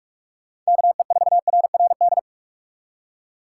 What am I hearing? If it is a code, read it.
Morse code, heard as a keyed tone, about 59 wpm, spaced as for 52 wpm, 700 Hz. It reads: KE4RRD
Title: Speed Racing - USA 2x3 Call Signs 45wpm